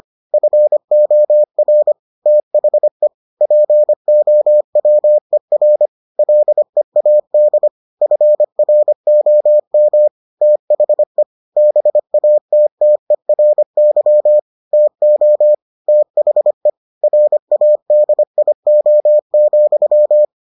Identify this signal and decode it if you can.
Morse code, heard as a keyed tone, about 25 wpm, 600 Hz. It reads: FOR THE POWER LEAD FROM THE BATTERY TO THE RADIO,